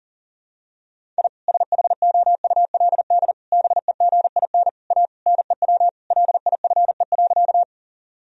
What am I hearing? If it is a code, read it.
Morse code, heard as a keyed tone, about 37 wpm, 700 Hz. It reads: I SHOULD BEGIN A NEW LIFE.
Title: Sentences from Top 1000 Words - Review Part 1 - 40wpm